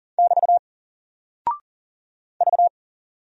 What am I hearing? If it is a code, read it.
Morse code, heard as a keyed tone, about 40 wpm, 700 Hz. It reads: <BT> V
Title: Single Character (Letter:Number:Punctuation:Prosign) 40wpm